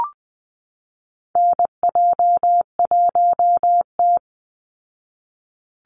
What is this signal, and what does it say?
Morse code, heard as a keyed tone, about 19 wpm, 700 Hz. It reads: NJ1T
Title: Call Signs Repeated 3x - 20wpm